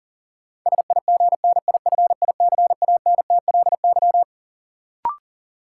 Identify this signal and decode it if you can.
Morse code, heard as a keyed tone, about 40 wpm, 700 Hz. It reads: SIGNIFICANTLY
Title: Sentences from Top 1500 Words - New Word Drill - 40wpm